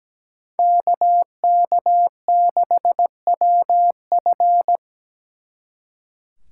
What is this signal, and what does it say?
Morse code, heard as a keyed tone, about 20 wpm, 700 Hz. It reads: KK6WF
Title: Speed Racing - USA Call Signs 17wpm